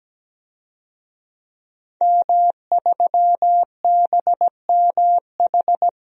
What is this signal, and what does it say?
Morse code, heard as a keyed tone, about 17 wpm, 700 Hz. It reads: M3BMH
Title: DXCC-100 Call Signs - 5 Characters - 17wpm